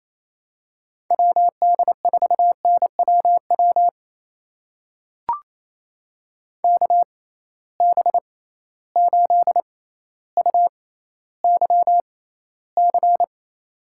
Morse code, heard as a keyed tone, about 28 wpm, spaced as for 14 wpm, 700 Hz. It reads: WD4NWW KB8UYC
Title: USA Call Signs - 6 Characters - 6x Character Spacing - 28wpm